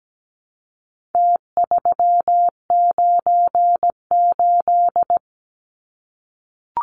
Morse code, heard as a keyed tone, about 17 wpm, 700 Hz. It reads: T398
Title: Numbers - Contest Style - 4-Digit 17wpm